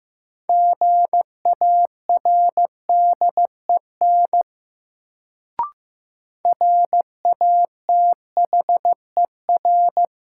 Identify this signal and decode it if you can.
Morse code, heard as a keyed tone, about 15 wpm, 700 Hz. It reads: GARDEN RATHER
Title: Sentences from Top 700 Words - New Word Drill - 15wpm